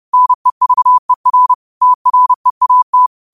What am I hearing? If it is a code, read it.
Morse code, heard as a keyed tone, about 30 wpm, 1000 Hz. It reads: NEVER TREAT
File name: qotdfiles / 30wpm